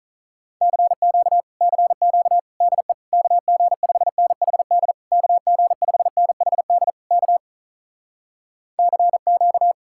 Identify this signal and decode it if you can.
Morse code, heard as a keyed tone, about 48 wpm, 700 Hz. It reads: CQ CQ DE KG5NHD KG5NHD K CQ